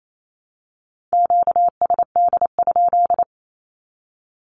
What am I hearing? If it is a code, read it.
Morse code, heard as a keyed tone, about 28 wpm, 700 Hz. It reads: QSD?
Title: Q-Codes Repeated 3x 28wpm